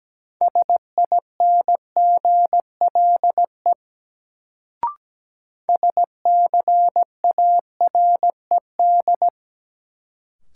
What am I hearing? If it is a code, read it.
Morse code, heard as a keyed tone, about 17 wpm, 700 Hz. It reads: SINGLE SCARED
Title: Sentences from Top 700 Words - New Word Drill - 17wpm